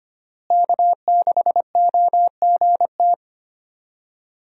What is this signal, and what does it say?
Morse code, heard as a keyed tone, about 25 wpm, 700 Hz. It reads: K6OGT
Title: USA Call Signs - 5 Characters - Repeated 3x - 25wpm